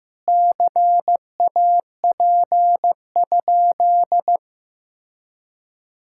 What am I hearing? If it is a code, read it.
Morse code, heard as a keyed tone, about 15 wpm, 700 Hz. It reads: CAP?
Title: Sentences from Top 1600 Words 15wpm